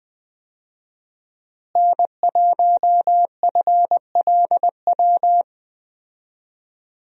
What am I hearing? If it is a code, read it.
Morse code, heard as a keyed tone, about 20 wpm, 700 Hz. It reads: N1FLW
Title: USA Call Signs - 5 Characters - Repeated 3x - 20wpm